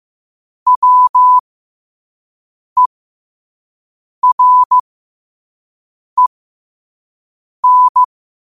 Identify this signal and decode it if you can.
Morse code, heard as a keyed tone, about 15 wpm, spaced as for 5 wpm, 1000 Hz. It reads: WEREN
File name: qotdfiles / 5wpm